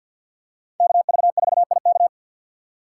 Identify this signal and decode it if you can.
Morse code, heard as a keyed tone, about 52 wpm, 700 Hz. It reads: KV4IK
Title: USA Call Signs - 6x Character Spacing - 50wpm